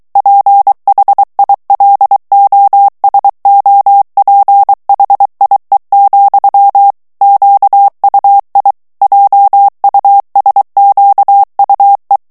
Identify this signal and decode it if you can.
Morse code, heard as a keyed tone, about 23 wpm, 800 Hz. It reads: PHILOSOPHIE, QUI JUSQUE
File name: FRANCAIS TEXTE CLAIR 2 VITESSE 22 MOTS MINUTES